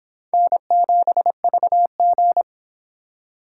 Morse code, heard as a keyed tone, about 25 wpm, 700 Hz. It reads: N7VG